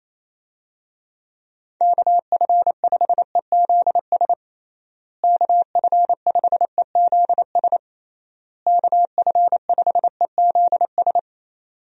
Morse code, heard as a keyed tone, about 28 wpm, 700 Hz. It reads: KF5EZS KF5EZS KF5EZS
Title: USA Call Signs - 6 Characters - Repeated 3x - 28wpm